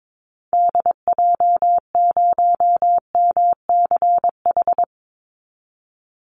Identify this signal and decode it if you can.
Morse code, heard as a keyed tone, about 22 wpm, 700 Hz. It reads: DJ0MCH
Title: Call Signs Repeated 3x - 22wpm